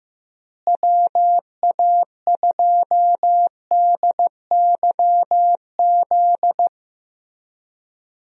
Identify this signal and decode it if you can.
Morse code, heard as a keyed tone, about 15 wpm, 700 Hz. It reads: WA2DYZ
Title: USA Call Signs - 6 Characters - Repeated 3x - 15wpm